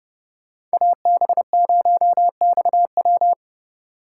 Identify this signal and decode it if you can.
Morse code, heard as a keyed tone, about 34 wpm, 700 Hz. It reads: AB0XW